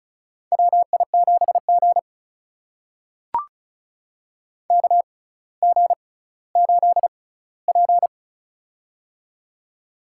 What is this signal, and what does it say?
Morse code, heard as a keyed tone, about 36 wpm, spaced as for 18 wpm, 700 Hz. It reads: WI7G KG8P
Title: USA Call Signs - 6x Character Spacing - 35wpm